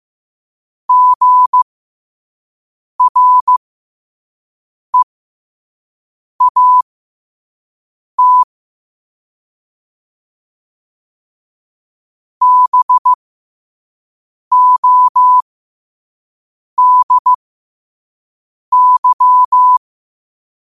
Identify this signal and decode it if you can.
Morse code, heard as a keyed tone, about 15 wpm, spaced as for 5 wpm, 1000 Hz. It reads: GREAT BODY
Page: New quotes every day in morse code at 5 Words per minute.